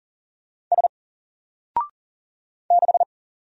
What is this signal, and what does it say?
Morse code, heard as a keyed tone, about 40 wpm, 700 Hz. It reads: S 6
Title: Single Letter-Number 40wpm